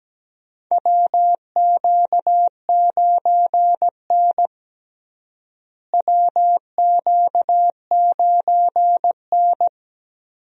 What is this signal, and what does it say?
Morse code, heard as a keyed tone, about 17 wpm, 700 Hz. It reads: WQ9N WQ9N
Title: USA Call Signs - 4 Characters - Repeated 2x - 17wpm